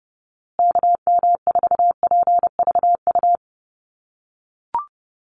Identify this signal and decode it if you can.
Morse code, heard as a keyed tone, about 30 wpm, 700 Hz. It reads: KM4PVU
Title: USA Call Signs Repeated 3x 30wpm